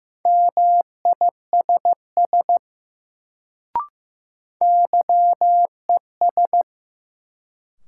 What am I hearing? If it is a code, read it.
Morse code, heard as a keyed tone, about 15 wpm, 700 Hz. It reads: MISS YES
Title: Sentences from Top 500 Words - New Word Drill - 15wpm